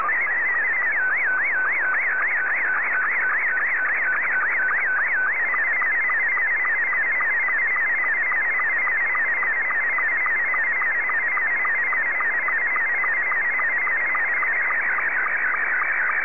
SSTV (slow-scan television), Scottie 2.